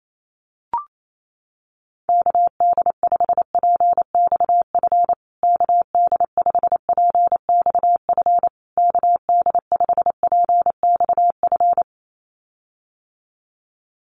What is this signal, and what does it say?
Morse code, heard as a keyed tone, about 28 wpm, 700 Hz. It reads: KD5PXF KD5PXF KD5PXF
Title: USA Call Signs Repeated 3x 28wpm